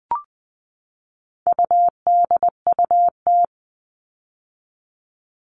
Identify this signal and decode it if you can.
Morse code, heard as a keyed tone, about 20 wpm, 700 Hz. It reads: UDUT
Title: All Cut Numbers - 4-Digit 20wpm